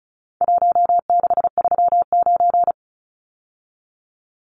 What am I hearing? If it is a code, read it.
Morse code, heard as a keyed tone, about 34 wpm, 700 Hz. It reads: 1639
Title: Numbers - Contest Style - 4-Digit 35wpm